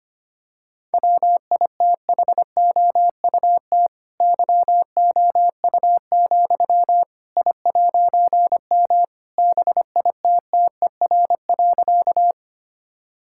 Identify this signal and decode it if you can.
Morse code, heard as a keyed tone, about 25 wpm, 700 Hz. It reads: WITHOUT YOU, I'M BITTER.
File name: practice / 25wpm / All-Age Appropriate Jokes - 25wpm